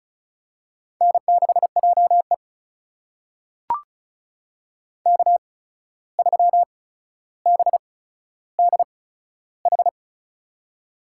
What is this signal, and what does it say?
Morse code, heard as a keyed tone, about 34 wpm, spaced as for 14 wpm, 700 Hz. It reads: N6JE K3BDH
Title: USA Call Signs - 8x Character Spacing - 35wpm